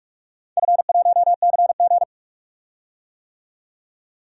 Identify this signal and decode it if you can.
Morse code, heard as a keyed tone, about 42 wpm, 700 Hz. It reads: F1CG
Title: DXCC-100 Call Signs - 4 Characters - 45wpm